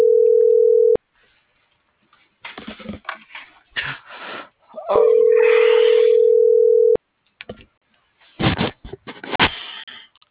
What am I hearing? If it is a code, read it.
Ringback tone (North America), 440 + 480 Hz.